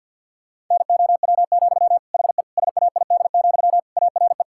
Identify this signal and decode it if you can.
Morse code, heard as a keyed tone, about 51 wpm, 700 Hz. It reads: NOW, HE SAID, ARE
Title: Sentences from Top 1600 Words 50wpm